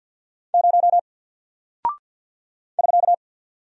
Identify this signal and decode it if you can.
Morse code, heard as a keyed tone, about 49 wpm, 700 Hz. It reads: 0 <SK>
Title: Single Character (Letter:Number:Punctuation:Prosign) 50wpm